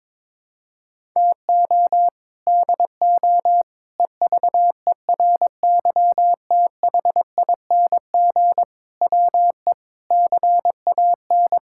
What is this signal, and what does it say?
Morse code, heard as a keyed tone, about 23 wpm, 700 Hz. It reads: TO DO EVERYTHING WE CAN
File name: practice / 22wpm / Sets of 6 Words - 22wpm